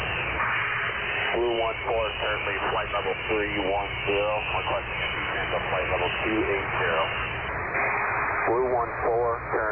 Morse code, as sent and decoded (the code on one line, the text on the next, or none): none